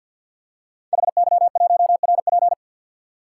Morse code, heard as a keyed tone, about 50 wpm, 700 Hz. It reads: HY1RP
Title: Code Groups - 5 Letter:Numbers - 50wpm